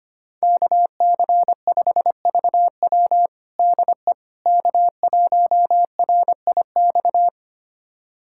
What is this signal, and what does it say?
Morse code, heard as a keyed tone, about 25 wpm, 700 Hz. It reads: KC5VW DE K1RIX
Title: USA Callsign DE Callsign 25wpm